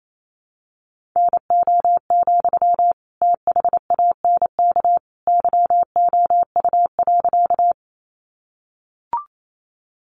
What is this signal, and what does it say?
Morse code, heard as a keyed tone, about 28 wpm, 700 Hz. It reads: NO, THANK YOU.
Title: Sentences from Top 600 Words 28wpm